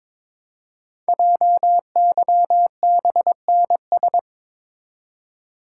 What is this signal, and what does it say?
Morse code, heard as a keyed tone, about 22 wpm, 700 Hz. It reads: JYBNS
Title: Code Groups - 5 Letters - 22wpm